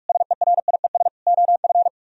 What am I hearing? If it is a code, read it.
Morse code, heard as a keyed tone, about 62 wpm, spaced as for 53 wpm, 700 Hz. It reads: SERIES OF
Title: Speed-Racing - Sets of 3 Words 45wpm